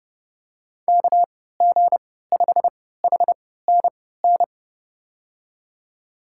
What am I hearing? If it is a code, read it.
Morse code, heard as a keyed tone, about 30 wpm, spaced as for 17 wpm, 700 Hz. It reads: KG5HNN
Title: USA Call Signs - 3x Character Spacing - 30wpm